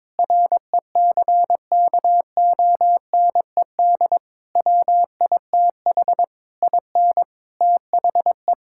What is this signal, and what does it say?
Morse code, heard as a keyed tone, about 22 wpm, 700 Hz. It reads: RECKONED WITH IN THE
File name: practice / 22wpm / Sentences with 8-Word Phrases - 22wpm